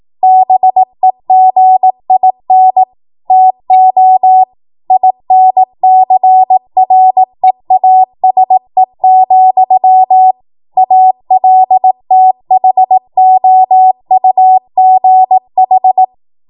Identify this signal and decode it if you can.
Morse code, heard as a keyed tone, about 18 wpm, 750 Hz. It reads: BEGIN TO INCREASE, ALTHOUGH
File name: Morse / 18WPM